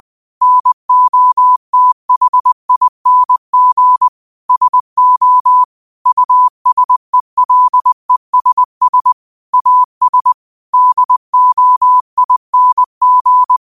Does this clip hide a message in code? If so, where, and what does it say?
Yes: Morse code from 0.4 to 13.6 s, heard as a keyed tone, about 20 wpm, 1000 Hz. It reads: NOTHING SO USELESS AS DOING